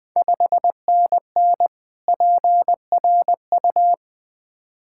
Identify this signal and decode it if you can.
Morse code, heard as a keyed tone, about 20 wpm, 700 Hz. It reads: 5NN PRU
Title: Ontario QSO Party Exchange 20wpm